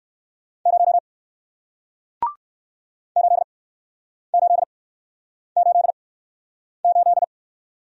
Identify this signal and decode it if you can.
Morse code, heard as a keyed tone, about 60 wpm, spaced as for 9 wpm, 700 Hz. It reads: 0 8888